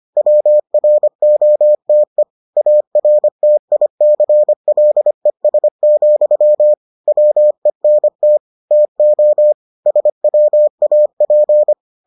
Morse code, heard as a keyed tone, about 25 wpm, 600 Hz. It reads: WROTE ARTICLES, WENT TO SWAP